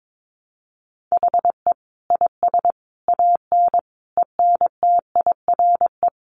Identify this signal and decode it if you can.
Morse code, heard as a keyed tone, about 22 wpm, 700 Hz. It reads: HE IS AN ENTIRE
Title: A Sound Of Thunder 22wpm